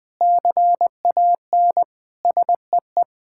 Morse code, heard as a keyed tone, about 20 wpm, 700 Hz. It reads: CAN SEE
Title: Sets of 9 Words - 20wpm